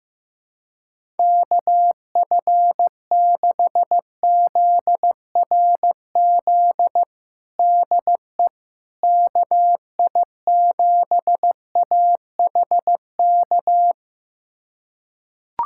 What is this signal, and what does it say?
Morse code, heard as a keyed tone, about 15 wpm, 700 Hz. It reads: KF6ZRZ DE KI7AHK
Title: USA Callsign DE Callsign 15wpm